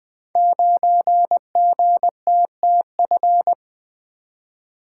Morse code, heard as a keyed tone, about 20 wpm, 700 Hz. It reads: 9GTTF
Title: Code Groups - 5 Letter:Numbers - 20wpm